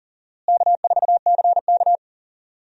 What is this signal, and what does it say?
Morse code, heard as a keyed tone, about 38 wpm, 700 Hz. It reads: K4CK
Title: USA Call Signs - 4 Characters - Repeated 2x - 40wpm